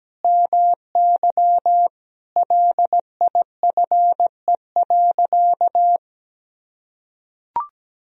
Morse code, heard as a keyed tone, about 17 wpm, 700 Hz. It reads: MY LIFE.
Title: Sentences from Top 1000 Words - Review Part 3 - 17wpm